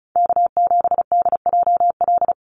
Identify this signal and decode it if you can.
Morse code, heard as a keyed tone, about 34 wpm, 700 Hz. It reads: K7DJL
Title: NA SOTA Chaser Callsigns 35wpm